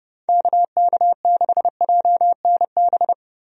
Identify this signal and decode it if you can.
Morse code, heard as a keyed tone, about 31 wpm, 700 Hz. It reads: KK6JNB
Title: USA Call Signs - 6 Characters - Repeated 3x - 30wpm